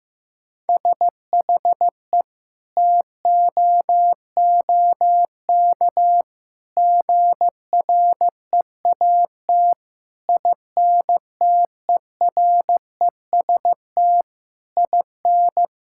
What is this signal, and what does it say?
Morse code, heard as a keyed tone, about 15 wpm, 700 Hz. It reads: SHE TOOK GREAT INTEREST IN
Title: Sentences from Top 1000 Words - Review Part 6 - 15wpm